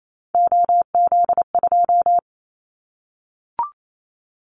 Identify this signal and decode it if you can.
Morse code, heard as a keyed tone, about 28 wpm, 700 Hz. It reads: OZ2
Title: Callsign Prefixes for Top 100 DXCC - 28wpm